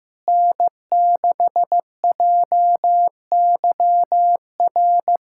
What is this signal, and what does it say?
Morse code, heard as a keyed tone, about 15 wpm, 700 Hz. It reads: N6JYR
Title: USA Callsign DE Callsign 15wpm